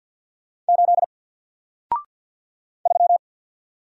Morse code, heard as a keyed tone, about 53 wpm, 700 Hz. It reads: 8 3
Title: Single Letter-Number 50wpm